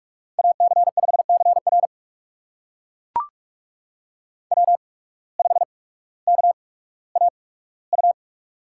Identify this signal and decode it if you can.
Morse code, heard as a keyed tone, about 44 wpm, spaced as for 19 wpm, 690 Hz. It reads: AC5CR W5KAU
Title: USA Call Signs - 5 Characters - 8x Character Spacing - 45wpm